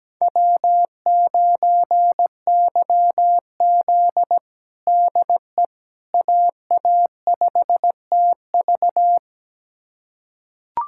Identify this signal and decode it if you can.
Morse code, heard as a keyed tone, about 17 wpm, 700 Hz. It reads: W9YZ DE AA5TV
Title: USA Callsign DE Callsign 17wpm